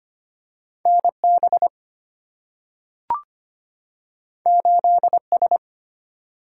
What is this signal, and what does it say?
Morse code, heard as a keyed tone, about 25 wpm, 700 Hz. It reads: NB 8S
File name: Two Letter-Number Combinations - 25wpm